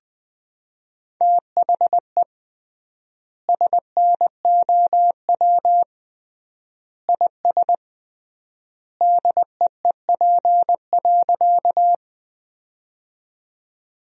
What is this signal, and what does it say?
Morse code, heard as a keyed tone, about 20 wpm, 700 Hz. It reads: THE SNOW IS DEEP.
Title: 3x Word-Spacing - Sentences from Top 500 Words - 20wpm